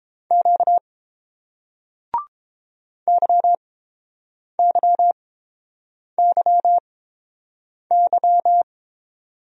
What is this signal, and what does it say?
Morse code, heard as a keyed tone, about 30 wpm, spaced as for 7 wpm, 690 Hz. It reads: Q YYYY